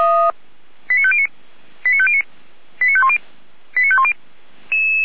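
Five-tone selective call (ZVEI), heard as quick runs of five tones, tones of about 70 ms. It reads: 89509 89509 89510 89510
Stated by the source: ZVEI five-tone selective call 89060 received on an SDR.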